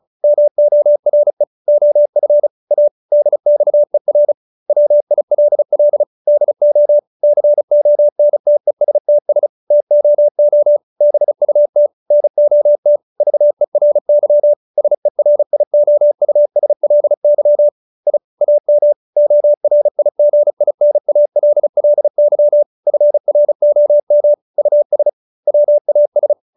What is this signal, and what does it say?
Morse code, heard as a keyed tone, about 34 wpm, 590 Hz. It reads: MORE OF A DXER WILL DO CONTESTS TOO BUT NOT VERY SERIOUSLY I AM ORIGINALLY FROM US WAS